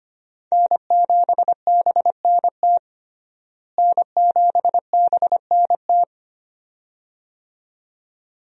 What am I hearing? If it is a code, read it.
Morse code, heard as a keyed tone, about 25 wpm, 700 Hz. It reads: N7BNT N7BNT
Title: USA Call Signs - 5 Characters - Repeated 2x - 25wpm